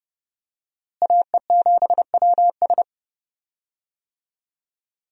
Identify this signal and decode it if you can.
Morse code, heard as a keyed tone, about 30 wpm, 700 Hz. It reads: AE7WS
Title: USA Call Signs - 5 Characters - Repeated 2x - 30wpm